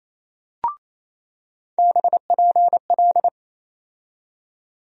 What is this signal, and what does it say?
Morse code, heard as a keyed tone, about 28 wpm, 700 Hz. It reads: BPL
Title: USA Call Sign Postfixes - 3 Letters 28wpm